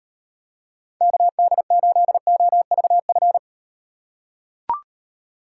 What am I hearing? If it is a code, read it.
Morse code, heard as a keyed tone, about 36 wpm, 700 Hz. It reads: KD8OVF